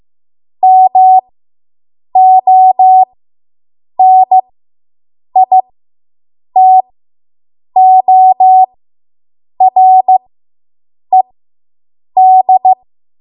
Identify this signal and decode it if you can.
Morse code, heard as a keyed tone, about 15 wpm, spaced as for 7 wpm, 750 Hz. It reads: MONITORED